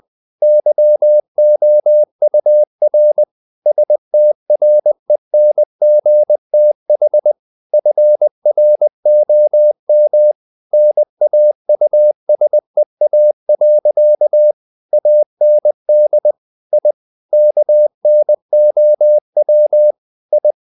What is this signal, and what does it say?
Morse code, heard as a keyed tone, about 20 wpm, 600 Hz. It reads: YOUR STRENGTH FROM NAUSEA. AND I KNOW I
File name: the old man and the sea part 2 20wpm